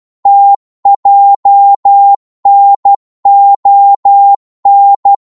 Morse code, heard as a keyed tone, about 12 wpm, 800 Hz. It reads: TJNON
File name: Morse / ltr-12wpm